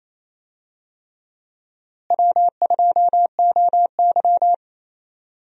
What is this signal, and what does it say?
Morse code, heard as a keyed tone, about 28 wpm, 700 Hz. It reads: W2OY
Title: USA Call Signs - 4 Characters - Repeated 3x - 28wpm